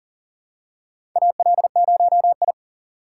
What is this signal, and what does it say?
Morse code, heard as a keyed tone, about 37 wpm, 700 Hz. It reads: AL0I
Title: USA Call Signs - 3x Character Spacing - 40wpm